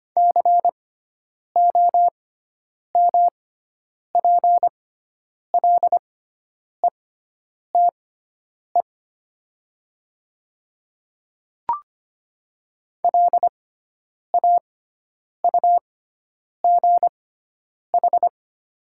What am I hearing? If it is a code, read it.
Morse code, heard as a keyed tone, about 25 wpm, spaced as for 9 wpm, 700 Hz. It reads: COMPLETE LAUGH
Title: 6x Character Spacing - Top 500 Words - 25wpm